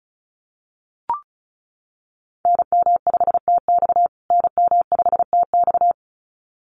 Morse code, heard as a keyed tone, about 34 wpm, 700 Hz. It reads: NM5TX NM5TX
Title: USA Call Signs Repeated 2x 35wpm